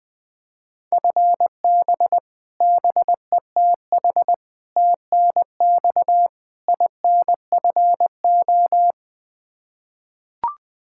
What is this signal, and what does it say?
Morse code, heard as a keyed tone, about 21 wpm, 700 Hz. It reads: FB BETH TNX INFO
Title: Practice QSO - Exchange 3 - More Info - 20wpm